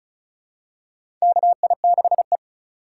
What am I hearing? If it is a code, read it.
Morse code, heard as a keyed tone, about 34 wpm, 700 Hz. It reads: KI6E
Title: USA Call Signs - 4 Characters - 2x Character Spacing - 35wpm